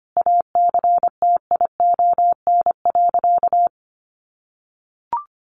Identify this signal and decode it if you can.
Morse code, heard as a keyed tone, about 25 wpm, 700 Hz. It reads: ACTION.
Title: Sentences from Top 1300 Words 25wpm